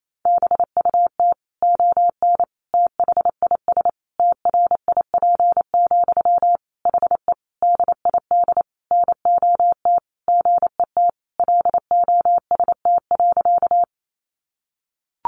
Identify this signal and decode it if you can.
Morse code, heard as a keyed tone, about 28 wpm, 700 Hz. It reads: BUT ON THIS TRIP, HE DID NOT GET LOST.
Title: Sentences from Top 1400 Words 28wpm